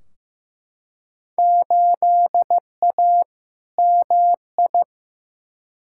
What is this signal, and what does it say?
Morse code, heard as a keyed tone, about 15 wpm, 700 Hz. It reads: 8A MI
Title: Field Day Exchanges Encore - 15wpm